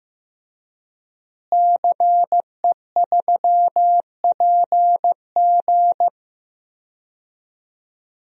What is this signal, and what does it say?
Morse code, heard as a keyed tone, about 15 wpm, 700 Hz. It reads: CE3PG
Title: DXCC-100 Call Signs - 5 Characters - Repeated 2x - 15wpm